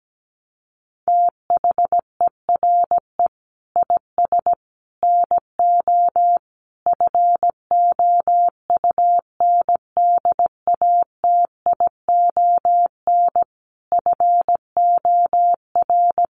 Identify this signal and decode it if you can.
Morse code, heard as a keyed tone, about 17 wpm, 700 Hz. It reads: THERE IS NO FOUNDATION FOR
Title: Sentences from Top 1200 Words 17wpm